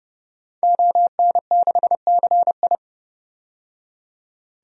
Morse code, heard as a keyed tone, about 30 wpm, 700 Hz. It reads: ON6CI
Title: DXCC-100 Call Signs - 5 Characters - 30wpm